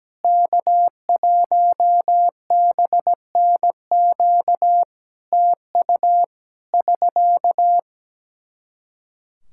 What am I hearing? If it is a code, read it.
Morse code, heard as a keyed tone, about 17 wpm, 700 Hz. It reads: K1BNQ TU <SK>
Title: Practice QSO - Exchange 4 - Ending - 17wpm